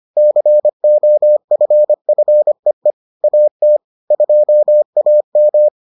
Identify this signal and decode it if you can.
Morse code, heard as a keyed tone, about 25 wpm, 600 Hz. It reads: COFFEE AT 2AM